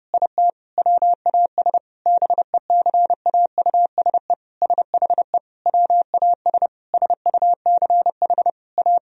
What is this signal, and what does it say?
Morse code, heard as a keyed tone, about 29 wpm, 700 Hz. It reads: IT WAS BECAUSE SHE WAS SUCH A
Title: Sentences from Top 1000 Words - Review Part 4 - 30wpm